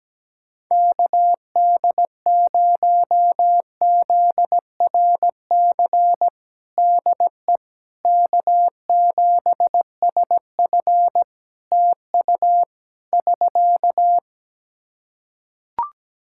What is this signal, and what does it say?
Morse code, heard as a keyed tone, about 17 wpm, 700 Hz. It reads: KD0ZRC DE K7SF TU <SK>
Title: Practice QSO - Exchange 4 - Ending - 17wpm